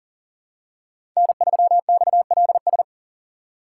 Morse code, heard as a keyed tone, about 40 wpm, 700 Hz. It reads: N3XLS
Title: K1USN SST QSO Practice - 40wpm